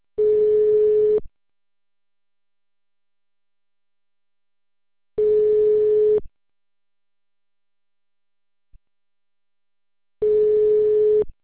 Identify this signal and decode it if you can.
Ringback tone (Europe, 425 Hz).